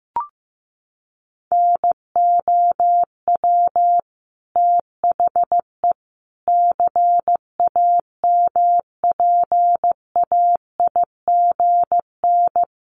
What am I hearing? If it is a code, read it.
Morse code, heard as a keyed tone, about 15 wpm, 700 Hz. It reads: NOW THE CAMPAIGN
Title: Sentences from Top 1200 Words 15wpm